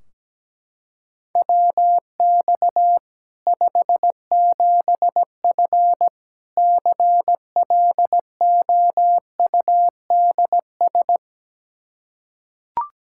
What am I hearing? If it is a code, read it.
Morse code, heard as a keyed tone, about 17 wpm, 700 Hz. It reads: WX 57F CLOUDS
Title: K1USN SST QSO Practice 17wpm